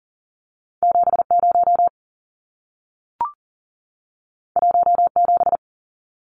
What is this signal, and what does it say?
Morse code, heard as a keyed tone, about 40 wpm, 700 Hz. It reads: 70 17
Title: Numbers - 2-Digit 40wpm